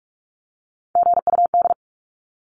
Morse code, heard as a keyed tone, about 44 wpm, 700 Hz. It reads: GUD
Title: Abbreviations 45wpm